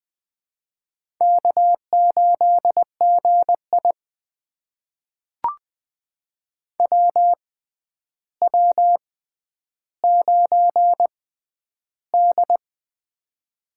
Morse code, heard as a keyed tone, about 21 wpm, spaced as for 10 wpm, 700 Hz. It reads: K8GI WW9D
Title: USA Call Signs - 4 Characters - 6x Character Spacing - 20wpm